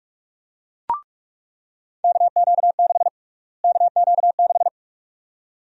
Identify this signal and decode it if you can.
Morse code, heard as a keyed tone, about 42 wpm, 700 Hz. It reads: KQ6 KQ6
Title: USA Call Sign Prefixes - Repeated 2x - 45wpm